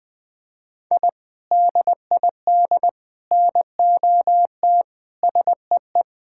Morse code, heard as a keyed tone, about 20 wpm, 700 Hz. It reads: I DID NOT SEE
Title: Sentences with 3-Word Phrases - 20wpm